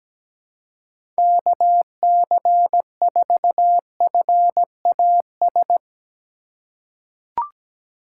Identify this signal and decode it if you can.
Morse code, heard as a keyed tone, about 17 wpm, 700 Hz. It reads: KC4FAS
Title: USA Call Signs - 4x Character Spacing - 17wpm